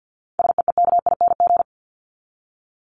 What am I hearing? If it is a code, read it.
Morse code, heard as a keyed tone, about 49 wpm, 700 Hz. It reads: SEEKING
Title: ING Suffix - Top 100 Words with Base Words 50wpm